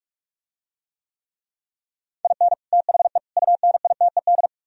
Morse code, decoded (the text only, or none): IN THE UNITED